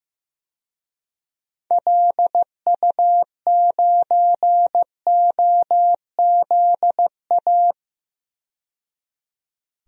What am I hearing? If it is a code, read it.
Morse code, heard as a keyed tone, about 15 wpm, 700 Hz. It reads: LU9OZA
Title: DXCC-100 Call Signs - 6 Characters - 15wpm